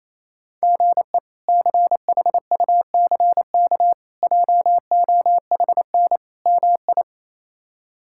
Morse code, heard as a keyed tone, about 28 wpm, 700 Hz. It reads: GE CHUCK JOHN MI
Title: K1USN SST QSO Practice - 28wpm